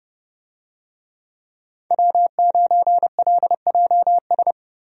Morse code, heard as a keyed tone, about 30 wpm, 700 Hz. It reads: W9LJS
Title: USA Call Signs - 5 Characters - Repeated 3x - 30wpm